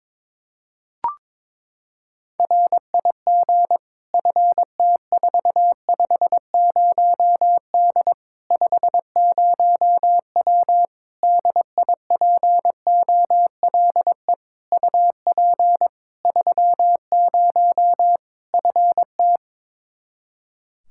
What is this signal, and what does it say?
Morse code, heard as a keyed tone, about 22 wpm, 700 Hz. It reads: RIG FT450D 50W DIPOLE UP 30 FT
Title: Practice QSO - Exchange 2 - Info - 22wpm